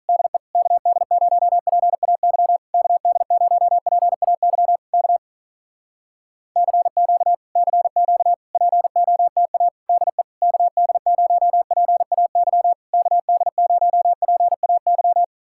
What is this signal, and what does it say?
Morse code, heard as a keyed tone, about 46 wpm, 700 Hz. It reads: DE KD0PAY KD0PAY K CQ CQ POTA DE KD0PAY KD0PAY